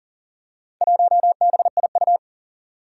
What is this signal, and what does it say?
Morse code, heard as a keyed tone, about 37 wpm, 700 Hz. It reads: 1BIU
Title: Four Challenging Letter-Number Combinations 40wpm